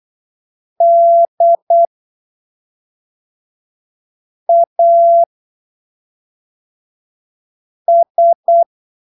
Morse code, heard as a keyed tone, about 8 wpm, spaced as for 3 wpm, 680 Hz. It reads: DAS
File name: arquivos / texto / texto 23 08ppm